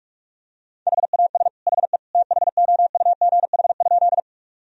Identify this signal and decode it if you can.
Morse code, heard as a keyed tone, about 41 wpm, 700 Hz. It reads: HAS HE THOUGH?
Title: Sentences from Top 1000 Words - Review Part 2 - 45wpm